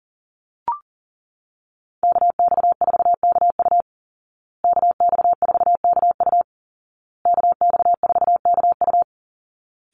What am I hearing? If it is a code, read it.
Morse code, heard as a keyed tone, about 40 wpm, 700 Hz. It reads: KX4KU KX4KU KX4KU
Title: Call Signs Repeated 2x - 40wpm